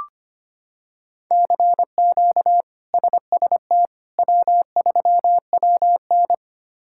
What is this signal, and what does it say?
Morse code, heard as a keyed tone, about 25 wpm, 700 Hz. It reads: CQ SST W3WN
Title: K1USN SST QSO Practice - 25wpm